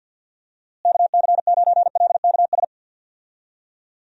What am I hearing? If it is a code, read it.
Morse code, heard as a keyed tone, about 52 wpm, 700 Hz. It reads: KC9LKS
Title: USA Call Signs - 6 Characters - Repeated 3x - 50wpm